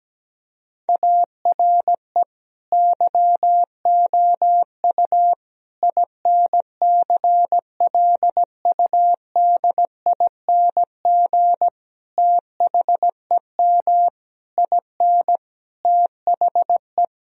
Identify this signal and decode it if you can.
Morse code, heard as a keyed tone, about 17 wpm, 700 Hz. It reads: ARE YOU INCLUDING THEM IN THE
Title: Sentences from Top 1000 Words - Review Part 4 - 17wpm